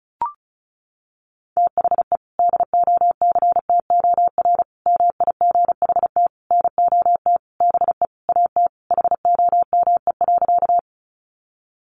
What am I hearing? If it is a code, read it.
Morse code, heard as a keyed tone, about 34 wpm, 700 Hz. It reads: THE DOCTOR MIGHT NOT BE AT HOME.
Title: Sentences from Top 1500 Words 35wpm